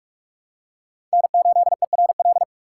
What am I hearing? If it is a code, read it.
Morse code, heard as a keyed tone, about 42 wpm, 700 Hz. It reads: N8ERL
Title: USA Call Signs - 5 Characters - 4x Character Spacing - 45wpm